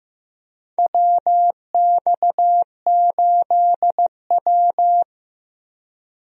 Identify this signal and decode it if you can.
Morse code, heard as a keyed tone, about 15 wpm, 700 Hz. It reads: WX8W
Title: USA Call Signs - 4 Characters - Repeated 3x - 15wpm